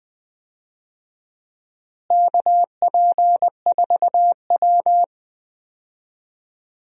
Morse code, heard as a keyed tone, about 20 wpm, 700 Hz. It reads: KP4W
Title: DXCC-100 Call Signs - 4 Characters - Repeated 2x - 20wpm